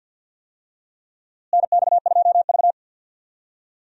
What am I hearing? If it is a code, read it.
Morse code, heard as a keyed tone, about 47 wpm, 700 Hz. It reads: NX2V
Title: USA Call Signs - 4 Characters - Repeated 3x - 50wpm